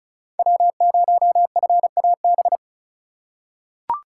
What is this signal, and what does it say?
Morse code, heard as a keyed tone, about 36 wpm, 700 Hz. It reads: W0FAB
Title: USA Call Signs - 5 Characters - 2x Character Spacing - 35wpm